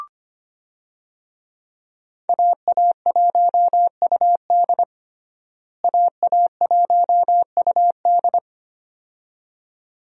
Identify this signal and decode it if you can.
Morse code, heard as a keyed tone, about 25 wpm, 700 Hz. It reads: AA1UD AA1UD
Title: USA Call Signs - 5 Characters - Repeated 2x - 25wpm